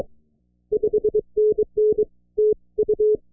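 Morse code, heard as a keyed tone, about 23 wpm, 430 Hz. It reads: 5NN TU
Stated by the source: Live operation from a new contest station in the Costa Rica mountains near the Continental Divide, with an open shot into Europe and great conditions.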